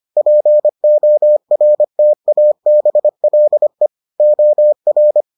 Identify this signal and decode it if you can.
Morse code, heard as a keyed tone, about 25 wpm, 600 Hz. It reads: PORTABLE OR